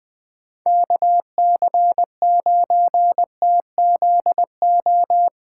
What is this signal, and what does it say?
Morse code, heard as a keyed tone, about 21 wpm, 700 Hz. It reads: KC9TZO
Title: K1USN SST QSO Practice - 20wpm